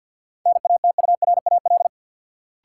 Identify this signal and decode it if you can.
Morse code, heard as a keyed tone, about 51 wpm, 700 Hz. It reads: NATURAL
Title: Sentences from Top 700 Words - New Word Drill - 50wpm